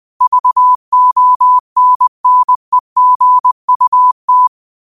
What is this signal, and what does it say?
Morse code, heard as a keyed tone, about 20 wpm, 1000 Hz. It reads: VONNEGUT
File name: qotdfiles / 20wpm